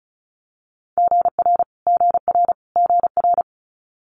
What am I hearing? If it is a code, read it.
Morse code, heard as a keyed tone, about 36 wpm, 700 Hz. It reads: GR GR GR
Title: Multi-letter Phonemes Repeated 3x 35wpm